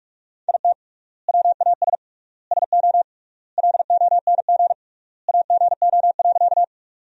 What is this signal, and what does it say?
Morse code, heard as a keyed tone, about 42 wpm, 700 Hz. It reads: IT WAS SO LONG AGO.
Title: 3x Word-Spacing - Sentences from Top 500 Words - 45wpm